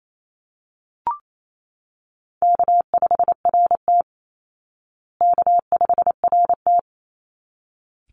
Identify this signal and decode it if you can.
Morse code, heard as a keyed tone, about 28 wpm, 700 Hz. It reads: K5RT K5RT
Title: Call Signs Repeated 3x - 28wpm